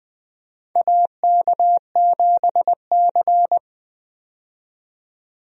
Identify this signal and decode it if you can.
Morse code, heard as a keyed tone, about 20 wpm, 700 Hz. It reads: AK7C
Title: USA Call Signs - 4 Characters - 20wpm